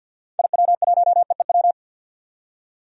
Morse code, heard as a keyed tone, about 52 wpm, 700 Hz. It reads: IW1EEW
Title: DXCC-100 Call Signs - 6 Characters - 50wpm